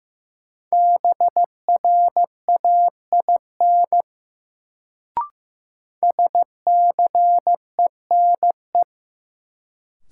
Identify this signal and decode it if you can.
Morse code, heard as a keyed tone, about 16 wpm, 700 Hz. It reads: BRAIN SCENE
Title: Sentences from Top 1100 Words - New Word Drill Alt - 15wpm